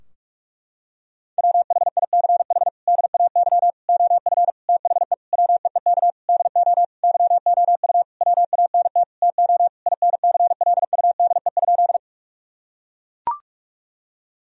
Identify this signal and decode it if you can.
Morse code, heard as a keyed tone, about 41 wpm, 700 Hz. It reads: WHICH DAY OF THE WEEK DO YOU WANT TO INCLUDE?
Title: Sentences from Top 1000 Words - Review Part 3 - 45wpm